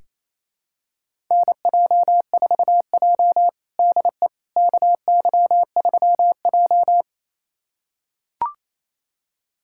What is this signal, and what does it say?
Morse code, heard as a keyed tone, about 28 wpm, 700 Hz. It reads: NJ4J DE KY3J
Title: USA Callsign DE Callsign - 4 Characters - 28wpm